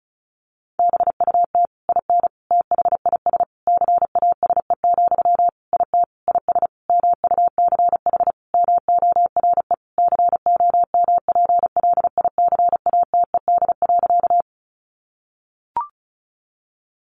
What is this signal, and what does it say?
Morse code, heard as a keyed tone, about 34 wpm, 700 Hz. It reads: BUT IN THIS CASE, IT IS MUCH MORE COMPLICATED.
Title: Sentences from Top 1200 Words 35wpm